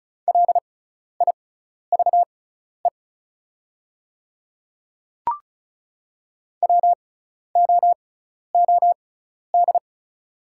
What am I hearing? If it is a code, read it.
Morse code, heard as a keyed tone, about 36 wpm, spaced as for 12 wpm, 700 Hz. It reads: LIVE WOOD
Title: 6x Character Spacing - Top 300 Words - 35wpm